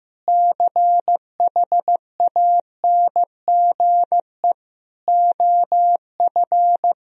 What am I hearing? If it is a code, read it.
Morse code, heard as a keyed tone, about 15 wpm, 700 Hz. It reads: CHANGE OF
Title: Sentences from Top 1000 Words - Review Part 4 - 15wpm